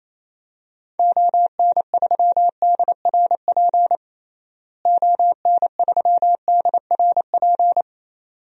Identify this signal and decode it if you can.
Morse code, heard as a keyed tone, about 28 wpm, 700 Hz. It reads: ON3DRP ON3DRP
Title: DXCC-100 Call Signs - 6 Characters - Repeated 2x - 28wpm